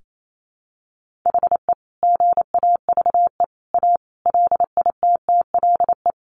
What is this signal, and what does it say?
Morse code, heard as a keyed tone, about 28 wpm, 700 Hz. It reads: HE GAVE A LITTLE
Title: Sentences from Top 1400 Words 28wpm